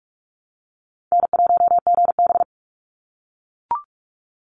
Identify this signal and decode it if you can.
Morse code, heard as a keyed tone, about 44 wpm, 700 Hz. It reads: N1GB
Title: NA SOTA Chaser Callsigns 45wpm